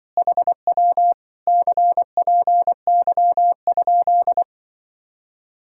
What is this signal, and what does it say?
Morse code, heard as a keyed tone, about 24 wpm, 700 Hz. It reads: HW CPY?
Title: Speed-Racing Common QSO Phrases Encore - 20wpm